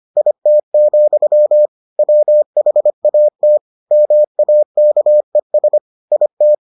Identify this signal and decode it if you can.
Morse code, heard as a keyed tone, about 25 wpm, 600 Hz. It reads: IT, WHAT MAKES IT